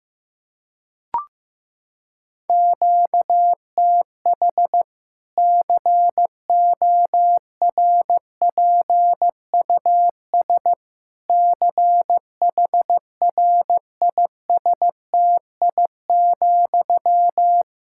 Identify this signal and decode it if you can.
Morse code, heard as a keyed tone, about 15 wpm, 700 Hz. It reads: QTH CORPUS CHRISTI,
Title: Practice QSO - Exchange 1 - Beginning - 15wpm